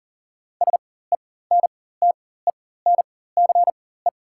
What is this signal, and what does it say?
Morse code, heard as a keyed tone, about 40 wpm, spaced as for 19 wpm, 700 Hz. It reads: SENTENCE
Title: 4x Character Spacing - Top 200 Words - 40wpm